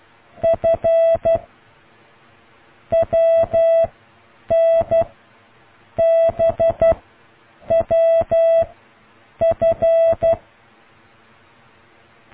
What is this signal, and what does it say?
Morse code, heard as a keyed tone, about 12 wpm, 660 Hz. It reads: F W N B W F